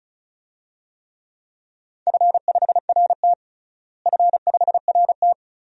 Morse code, heard as a keyed tone, about 36 wpm, 700 Hz. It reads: F5RT F5RT
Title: DXCC-100 Call Signs - 4 Characters - Repeated 2x - 35wpm